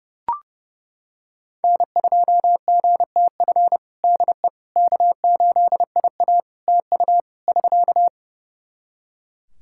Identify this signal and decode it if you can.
Morse code, heard as a keyed tone, about 30 wpm, 700 Hz. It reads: N2GTF DE K8IA TU <SK>
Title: Practice QSO - Exchange 4 - Ending - 30wpm